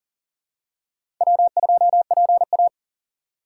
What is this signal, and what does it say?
Morse code, heard as a keyed tone, about 40 wpm, 700 Hz. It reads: W2PA
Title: USA Call Signs - 4 Characters - 3x Character Spacing - 40wpm